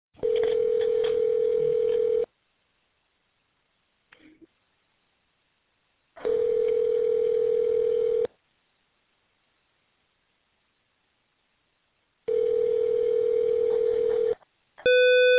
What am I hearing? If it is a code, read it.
Ringback tone (North America), 440 + 480 Hz.